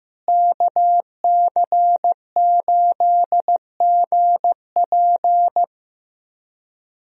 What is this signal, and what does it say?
Morse code, heard as a keyed tone, about 15 wpm, 700 Hz. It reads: KC8GP
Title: USA Call Signs - 5 Characters - Repeated 3x - 15wpm